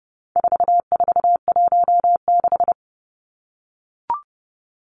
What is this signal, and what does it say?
Morse code, heard as a keyed tone, about 30 wpm, 700 Hz. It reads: 4416
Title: Numbers - 4-Digit 30wpm